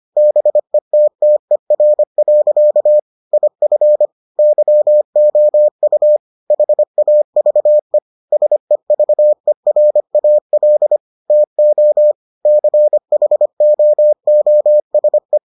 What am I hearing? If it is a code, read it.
Morse code, heard as a keyed tone, about 25 wpm, 600 Hz. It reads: BETTER. IF YOU HAVE SEVERAL TO CHOOSE